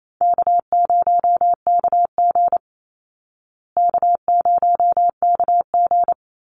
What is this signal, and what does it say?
Morse code, heard as a keyed tone, about 28 wpm, 700 Hz. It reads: K0KG K0KG
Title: Easy Call Signs Repeated 2x - 28wpm